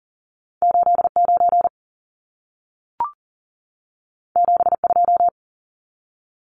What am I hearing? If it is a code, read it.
Morse code, heard as a keyed tone, about 40 wpm, 700 Hz. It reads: 89 72
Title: Numbers - 2-Digit 40wpm